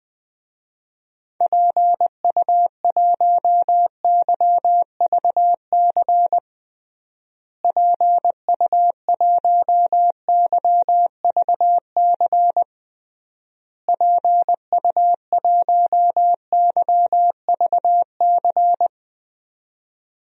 Morse code, heard as a keyed tone, about 20 wpm, 700 Hz. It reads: PU1YVC PU1YVC PU1YVC
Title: DXCC-100 Call Signs - 6 Characters - Repeated 3x - 20wpm